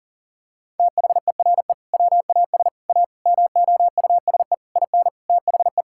Morse code, heard as a keyed tone, about 40 wpm, 700 Hz. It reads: THERE WAS A MOUSE IN THE
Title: Sentences from Top 1000 Words - Review Part 5 - 40wpm